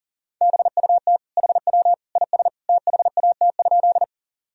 Morse code, heard as a keyed tone, about 40 wpm, 700 Hz. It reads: BUT HW IS THAT?
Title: Sentences from Top 100 Words and Abbreviations 40wpm